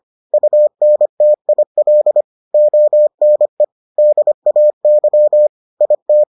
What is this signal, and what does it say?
Morse code, heard as a keyed tone, about 25 wpm, 600 Hz. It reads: UNTIL ONE DAY IT